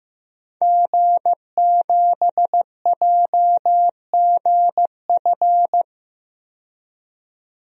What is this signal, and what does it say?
Morse code, heard as a keyed tone, about 15 wpm, 700 Hz. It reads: G7JGF
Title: DXCC-100 Call Signs - 5 Characters - 15wpm